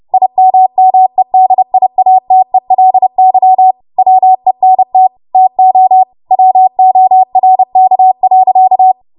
Morse code, heard as a keyed tone, about 30 wpm, 750 Hz. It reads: IMMEDIATELY WENT TO WORK.
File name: Morse / 30WPM